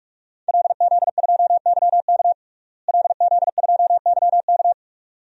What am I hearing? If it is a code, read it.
Morse code, heard as a keyed tone, about 46 wpm, 700 Hz. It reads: LZ2YK LZ2YK
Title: DXCC-100 Call Signs - 5 Characters - Repeated 2x - 45wpm